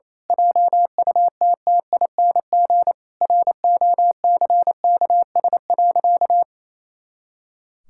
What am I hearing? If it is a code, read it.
Morse code, heard as a keyed tone, about 28 wpm, 700 Hz. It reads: JUTTING ROCKS.
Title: Pangram Challenge v2 - 28wpm